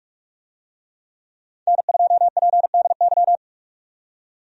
Morse code, heard as a keyed tone, about 44 wpm, 700 Hz. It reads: N2PDY
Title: USA Call Signs - 5 Characters - 45wpm